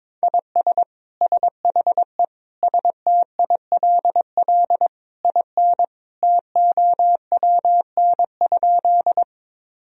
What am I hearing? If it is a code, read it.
Morse code, heard as a keyed tone, about 21 wpm, 700 Hz. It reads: IS SHE STILL IN TOWN?
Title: Sentences from Top 1000 Words - Review Part 2 - 22wpm